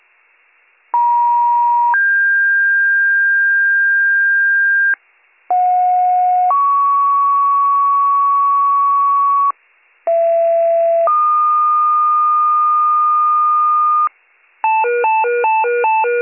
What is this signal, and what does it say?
Two-tone page, heard as a one-second tone, then a three-second tone, about 950 then 1600 Hz, about 730 then 1120 Hz, about 670 then 1220 Hz. It is heard from 0.9 to 14.1 s.